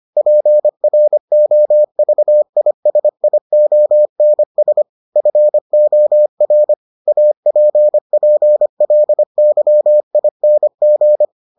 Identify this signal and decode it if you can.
Morse code, heard as a keyed tone, about 25 wpm, 600 Hz. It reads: PROVISIONS FOR APPLYING